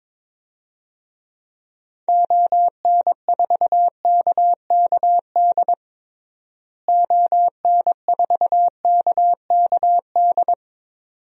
Morse code, heard as a keyed tone, about 22 wpm, 700 Hz. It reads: ON4KKD ON4KKD
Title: DXCC-100 Call Signs - 6 Characters - Repeated 2x - 22wpm